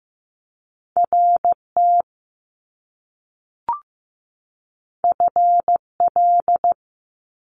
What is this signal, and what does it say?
Morse code, heard as a keyed tone, about 15 wpm, 700 Hz. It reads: RT FL
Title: Top Two Letter Combinations in Words 15wpm